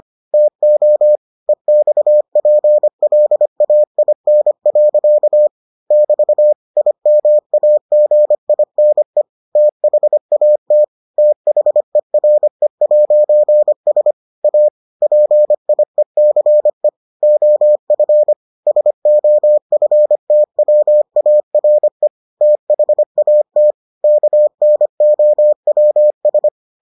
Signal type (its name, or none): Morse code